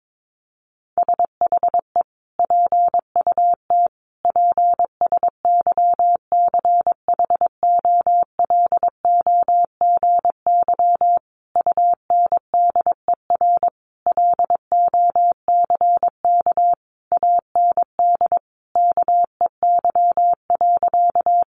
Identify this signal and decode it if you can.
Morse code, heard as a keyed tone, about 22 wpm, 700 Hz. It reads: SHE PUT PSYCHOLOGY UNDER LOCK AND KEY.
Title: Sentences from Top 1400 Words 22wpm